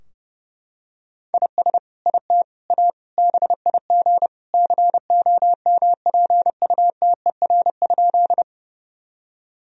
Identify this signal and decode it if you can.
Morse code, heard as a keyed tone, about 31 wpm, 700 Hz. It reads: IS IT A BIG COMPUTER?
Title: Sentences from Top 1000 Words - Review Part 3 - 30wpm